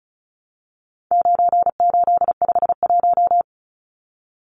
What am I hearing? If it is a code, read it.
Morse code, heard as a keyed tone, about 34 wpm, 700 Hz. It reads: 9851
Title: Numbers - 4-Digit 35wpm